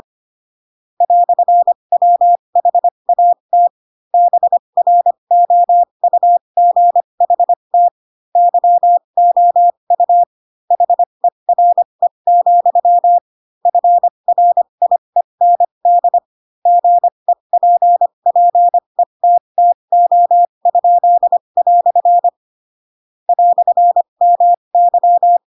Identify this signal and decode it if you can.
Morse code, heard as a keyed tone, about 25 wpm, spaced as for 22 wpm, 700 Hz. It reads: "WHAT BROUGHT YOU HERE, FRIEND GEPPETTO?" "MY
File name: Pinocchio / Pinocchio ch2 18 25 ik1hge